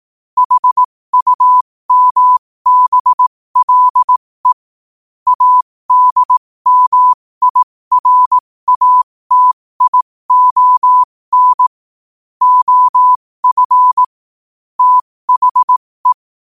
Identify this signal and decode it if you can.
Morse code, heard as a keyed tone, about 18 wpm, spaced as for 15 wpm, 1000 Hz. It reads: HUMBLE ADMIRATION OF THE
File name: qotdfiles / 15wpm